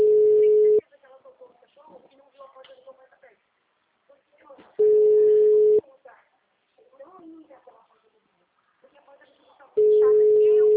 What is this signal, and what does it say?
Ringback tone (Europe, 425 Hz), with speech.